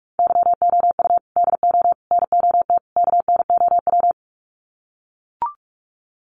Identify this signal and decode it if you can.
Morse code, heard as a keyed tone, about 44 wpm, 700 Hz. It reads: YOU DO NOT KNOW
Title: Sets of 4 Words 45wpm